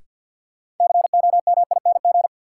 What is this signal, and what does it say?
Morse code, heard as a keyed tone, about 53 wpm, 700 Hz. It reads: COMING
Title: Sentences from Top 1300 Words - New Word Drill - 50wpm